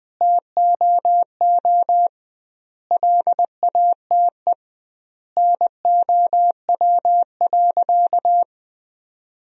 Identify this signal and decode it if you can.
Morse code, heard as a keyed tone, about 20 wpm, 700 Hz. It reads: TOO LATE NOW.
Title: 2x Word-Spacing - Sentences from Top 300 Words - 20wpm